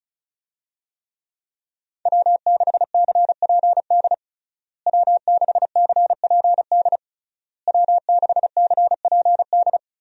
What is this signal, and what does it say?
Morse code, heard as a keyed tone, about 36 wpm, 700 Hz. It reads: W6CPD W6CPD W6CPD
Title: USA Call Signs - 5 Characters - Repeated 3x - 35wpm